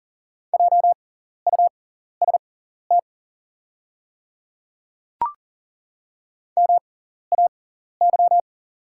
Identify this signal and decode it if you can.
Morse code, heard as a keyed tone, about 40 wpm, spaced as for 14 wpm, 700 Hz. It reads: JUST MAY
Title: 6x Character Spacing - Top 200 Words - 40wpm